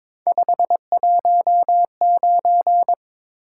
Morse code, heard as a keyed tone, about 22 wpm, 700 Hz. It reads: 519
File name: practice / 22wpm / Three Letter-Number Combinations - 22wpm